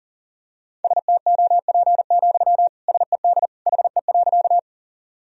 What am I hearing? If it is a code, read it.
Morse code, heard as a keyed tone, about 40 wpm, 700 Hz. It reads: STOP, SED HE.
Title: Sentences from Top 300 Words and Abbreviations 40wpm